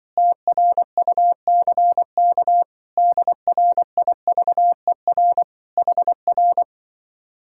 Morse code, heard as a keyed tone, about 24 wpm, 700 Hz. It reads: TRUCK DRIVER HR
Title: Speed-Racing Common QSO Phrases - 20wpm